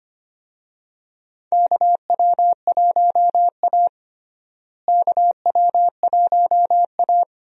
Morse code, heard as a keyed tone, about 25 wpm, 700 Hz. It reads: KW1A KW1A
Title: USA Call Signs - 4 Characters - Repeated 2x - 25wpm